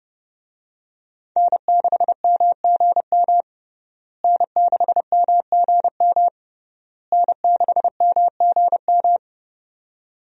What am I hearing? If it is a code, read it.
Morse code, heard as a keyed tone, about 32 wpm, 700 Hz. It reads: N6MGM N6MGM N6MGM
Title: USA Call Signs - 5 Characters - Repeated 3x - 30wpm